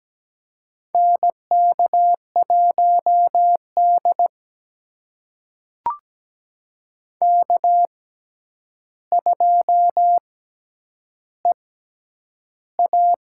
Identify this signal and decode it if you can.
Morse code, heard as a keyed tone, about 17 wpm, spaced as for 9 wpm, 700 Hz. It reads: NK1D K2EA
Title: USA Call Signs - 4 Characters - 6x Character Spacing - 17wpm